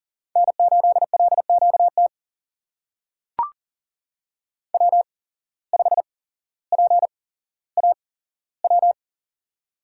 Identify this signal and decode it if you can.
Morse code, heard as a keyed tone, about 39 wpm, spaced as for 17 wpm, 700 Hz. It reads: N8LQT W5PAW
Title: USA Call Signs - 8x Character Spacing - 40wpm